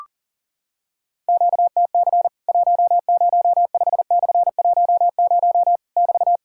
Morse code, heard as a keyed tone, about 40 wpm, 700 Hz. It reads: QTC 105/10 <BT>
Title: Worked All Europe DX Contest - QTC Exchange - 40wpm